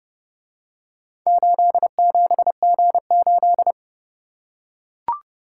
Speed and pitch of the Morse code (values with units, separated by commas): 28 wpm, 700 Hz